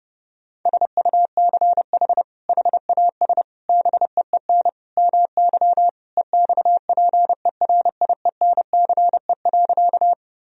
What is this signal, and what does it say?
Morse code, heard as a keyed tone, about 32 wpm, 700 Hz. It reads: SUCH HAS BEEN MY EXPERIENCE.
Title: Sentences from Top 1000 Words - Review Part 3 - 30wpm